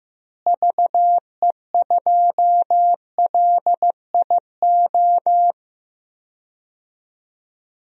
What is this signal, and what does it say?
Morse code, heard as a keyed tone, about 15 wpm, 700 Hz. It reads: VE2LIO
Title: DXCC-100 Call Signs - 6 Characters - Repeated 2x - 15wpm